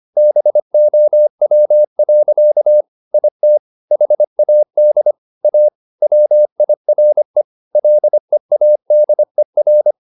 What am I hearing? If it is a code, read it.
Morse code, heard as a keyed tone, about 25 wpm, 600 Hz. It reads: BOW. IT HAD A WIRE LEADER
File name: the old man and the sea part 1 25wpm